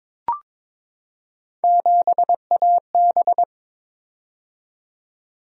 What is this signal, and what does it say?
Morse code, heard as a keyed tone, about 22 wpm, 700 Hz. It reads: 7AB
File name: practice / 22wpm / Three Letter-Number Combinations - 22wpm